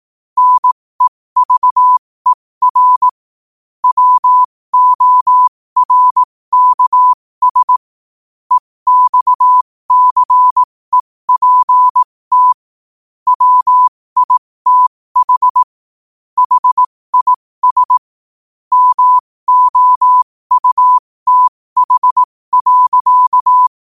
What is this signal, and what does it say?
Morse code, heard as a keyed tone, about 18 wpm, spaced as for 15 wpm, 1000 Hz. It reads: NEVER WORKS EXCEPT WITH HIS MOUTH.
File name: qotdfiles / 15wpm